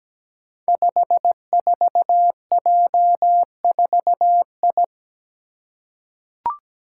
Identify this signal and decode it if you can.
Morse code, heard as a keyed tone, about 17 wpm, 700 Hz. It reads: 54J4I
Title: Five Challenging Letter-Number Combinations 17wpm